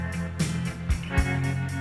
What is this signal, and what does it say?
There is no code.